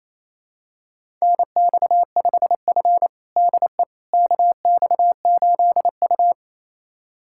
Morse code, heard as a keyed tone, about 28 wpm, 700 Hz. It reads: NX5F DE KX8U
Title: USA Callsign DE Callsign - 4 Characters - 28wpm